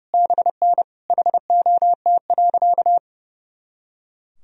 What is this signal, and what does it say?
Morse code, heard as a keyed tone, about 30 wpm, 700 Hz. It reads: BN HOT.
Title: Sentences from Top 500 Words and Abbreviations 30wpm